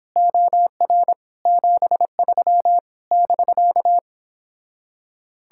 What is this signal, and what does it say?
Morse code, heard as a keyed tone, about 30 wpm, 700 Hz. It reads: OR 73 <BK>